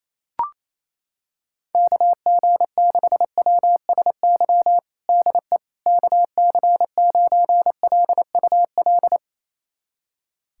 Morse code, heard as a keyed tone, about 28 wpm, 700 Hz. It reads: KG6WSY DE KC9LUL
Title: USA Callsign DE Callsign - 6 Characters - 28wpm